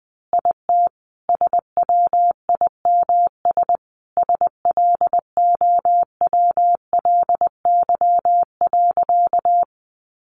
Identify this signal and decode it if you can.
Morse code, heard as a keyed tone, about 20 wpm, 700 Hz. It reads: IT SWIMS SLOWLY.
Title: Easy Phrases 20wpm